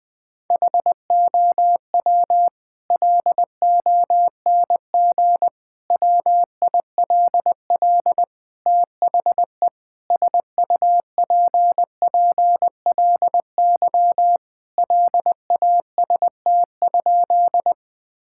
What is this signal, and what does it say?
Morse code, heard as a keyed tone, about 20 wpm, 700 Hz. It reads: HOW LONG WILL THE SUPPLY LAST?
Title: Sentences from Top 1000 Words - Review Part 5 - 20wpm